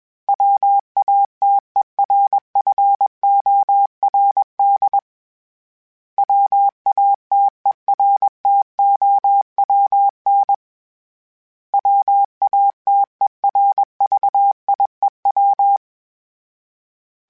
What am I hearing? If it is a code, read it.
Morse code, heard as a keyed tone, about 21 wpm, 800 Hz. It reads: WATERFORD WATERTOWN WATERVIEW
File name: fromSMs / FromSMs mp3s / Cities@20